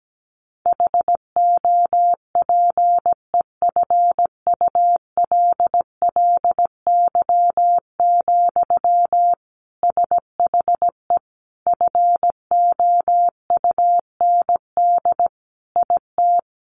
Morse code, heard as a keyed tone, about 17 wpm, 700 Hz. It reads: HOPEFULLY, SHE FOUND IT
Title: Sentences from Top 1500 Words 17wpm